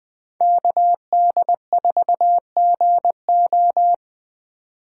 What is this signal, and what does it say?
Morse code, heard as a keyed tone, about 20 wpm, 700 Hz. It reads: KD4GO